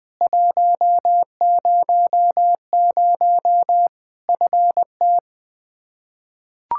Morse code, heard as a keyed tone, about 21 wpm, 700 Hz. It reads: 100 FT
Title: K1USN SST QSO Practice 20wpm